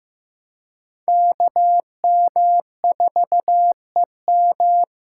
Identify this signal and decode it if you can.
Morse code, heard as a keyed tone, about 15 wpm, 700 Hz. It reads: KM4EM
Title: USA Call Signs - 5 Characters - 4x Character Spacing - 15wpm